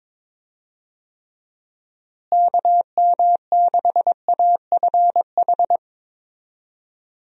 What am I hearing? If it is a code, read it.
Morse code, heard as a keyed tone, about 22 wpm, 700 Hz. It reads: KM6AFH
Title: USA Call Signs - 6 Characters - Repeated 2x - 22wpm